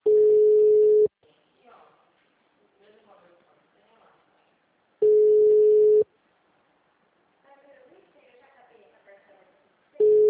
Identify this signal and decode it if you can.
Ringback tone (Europe, 425 Hz), with speech.
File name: out- -2011- - - .194